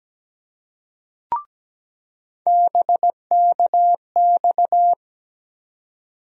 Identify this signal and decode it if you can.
Morse code, heard as a keyed tone, about 17 wpm, 700 Hz. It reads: BKX
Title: USA Call Sign Postfixes - 3 Letters 17wpm